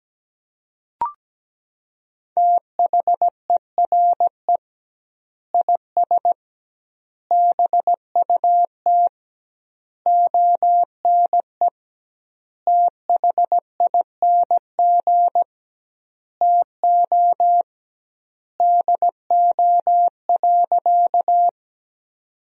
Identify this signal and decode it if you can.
Morse code, heard as a keyed tone, about 17 wpm, 700 Hz. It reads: THERE IS BUT ONE THING TO DO.
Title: 2x Word-Spacing - Sentences from Top 100 Words - 17wpm